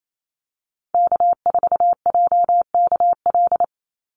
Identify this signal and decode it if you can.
Morse code, heard as a keyed tone, about 28 wpm, 700 Hz. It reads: K4JKL
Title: Call Signs Repeated 3x - 28wpm